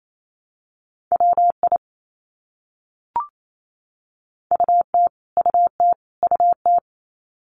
Morse code, heard as a keyed tone, about 28 wpm, 700 Hz. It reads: WI UT UT UT
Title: US States Repeated 3x 28wpm